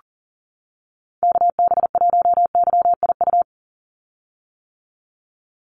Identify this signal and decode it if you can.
Morse code, heard as a keyed tone, about 40 wpm, 700 Hz. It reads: KB1YIU
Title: USA Call Signs 40wpm